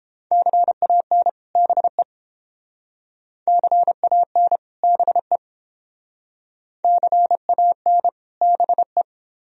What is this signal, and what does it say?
Morse code, heard as a keyed tone, about 30 wpm, 700 Hz. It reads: CAN BE CAN BE CAN BE